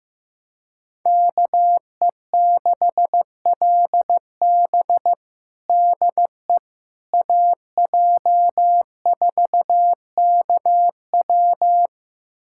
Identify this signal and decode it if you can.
Morse code, heard as a keyed tone, about 15 wpm, 700 Hz. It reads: KE6LB DE AJ4KW
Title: USA Callsign DE Callsign - 5 Characters - 15wpm